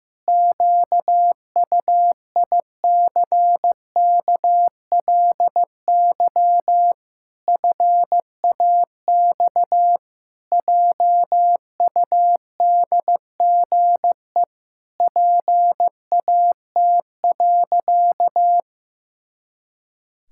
Morse code, heard as a keyed tone, about 15 wpm, 700 Hz. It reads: QUICKLY FAX JUDGE PAT.
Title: Pangram Challenge v2 - 15wpm